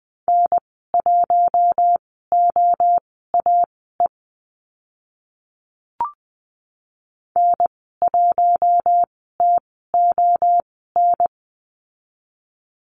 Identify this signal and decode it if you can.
Morse code, heard as a keyed tone, about 20 wpm, spaced as for 15 wpm, 700 Hz. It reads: N1OAE N1TON
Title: 069 Lesson 12g - Letter d - Call Signs 20wpm